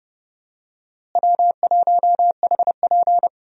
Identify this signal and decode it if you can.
Morse code, heard as a keyed tone, about 31 wpm, 700 Hz. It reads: W1HP
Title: USA Call Signs - 4 Characters - Repeated 3x - 30wpm